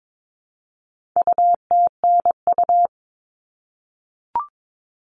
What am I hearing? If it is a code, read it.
Morse code, heard as a keyed tone, about 22 wpm, 700 Hz. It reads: UTNU
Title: All Cut Numbers - 4-Digit 22wpm